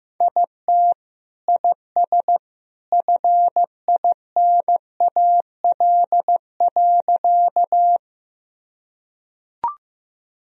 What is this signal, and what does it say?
Morse code, heard as a keyed tone, about 15 wpm, 700 Hz. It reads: IT IS FINAL.
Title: Sentences from Top 1000 Words - Review Part 2 - 15wpm